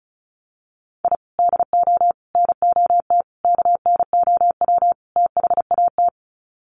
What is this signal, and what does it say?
Morse code, heard as a keyed tone, about 34 wpm, 700 Hz. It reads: I DO NOT KNOW THAT
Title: Sets of 5 Words 35wpm